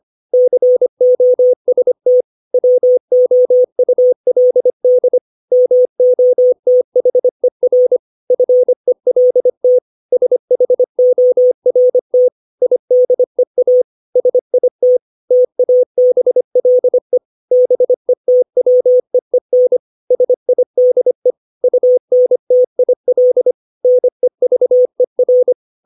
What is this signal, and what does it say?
Morse code, heard as a keyed tone, about 25 wpm, 500 Hz. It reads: COST WOULD MOTHER FELT SHORT IDEA SIT TABLE BETWEEN SIDE UNTIL NEVER
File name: 632 words / 632 Words 25WPM